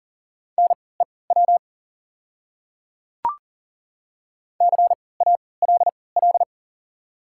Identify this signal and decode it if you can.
Morse code, heard as a keyed tone, about 40 wpm, spaced as for 23 wpm, 700 Hz. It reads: NEW CALL
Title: 3x Character Spacing - Top 500 Words - 40wpm